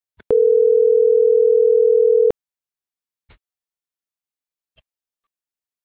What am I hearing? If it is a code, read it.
Ringback tone (North America), 440 + 480 Hz.